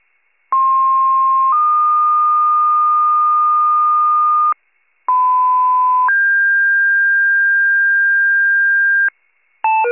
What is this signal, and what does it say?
Two-tone page, heard as a one-second tone, then a three-second tone, about 1060 then 1220 Hz, about 1010 then 1600 Hz, with noise. It is heard from 0.5 to 9.1 s.